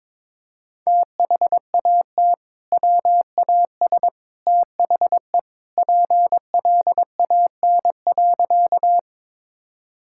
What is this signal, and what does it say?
Morse code, heard as a keyed tone, about 22 wpm, 700 Hz. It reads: THAT WAS THE PLAN.
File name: practice / 22wpm / Sentences with 2-Word Phrases - 22wpm